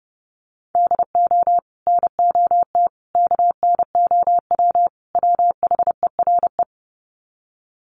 Morse code, heard as a keyed tone, about 30 wpm, 700 Hz. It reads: DO NOT KNOW WHERE
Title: Sets of 4 Words 30wpm